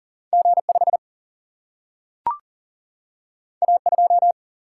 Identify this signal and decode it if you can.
Morse code, heard as a keyed tone, about 39 wpm, 700 Hz. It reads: G5 A2
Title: Single-Letter Single-Number 40wpm